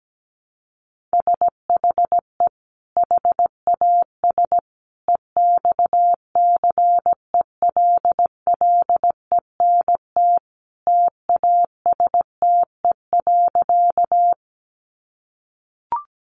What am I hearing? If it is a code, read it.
Morse code, heard as a keyed tone, about 17 wpm, 700 Hz. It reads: SHE HAS EXCELLENT TASTE.
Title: Sentences from Top 1100 Words 17wpm